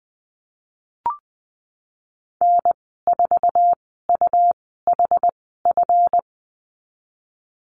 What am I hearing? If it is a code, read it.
Morse code, heard as a keyed tone, about 20 wpm, spaced as for 15 wpm, 700 Hz. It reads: N4UHF
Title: 140 Lesson 23g - Letter f - Call Signs 20wpm